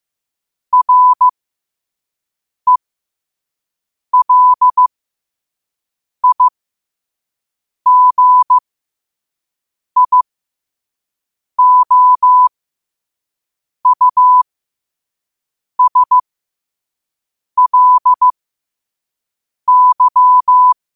Morse code, heard as a keyed tone, about 15 wpm, spaced as for 5 wpm, 1000 Hz. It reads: RELIGIOUSLY
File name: qotdfiles / 5wpm